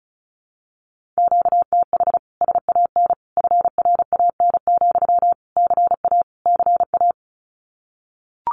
Morse code, heard as a keyed tone, about 34 wpm, 700 Hz. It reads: QTH SAN FRAN, CA CA
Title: Common QSO Phrases Encore 35wpm